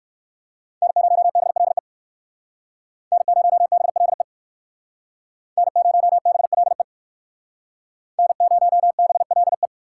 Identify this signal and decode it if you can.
Morse code, heard as a keyed tone, about 56 wpm, 700 Hz. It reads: N0BLE N0BLE N0BLE N0BLE
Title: Speed Racing - USA Call Signs 45wpm